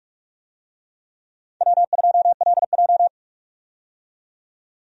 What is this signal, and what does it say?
Morse code, heard as a keyed tone, about 45 wpm, 700 Hz. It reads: W2LJ
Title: USA Call Signs - 4 Characters - 45wpm